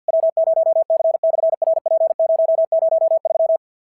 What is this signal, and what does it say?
Morse code, heard as a keyed tone, about 46 wpm, 660 Hz. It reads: W0C/RP003
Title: SOTA QSO Practice 50wpm